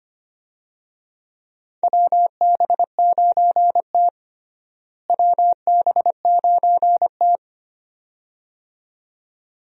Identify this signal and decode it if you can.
Morse code, heard as a keyed tone, about 25 wpm, 700 Hz. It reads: WB9T WB9T
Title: USA Call Signs - 4 Characters - Repeated 2x - 25wpm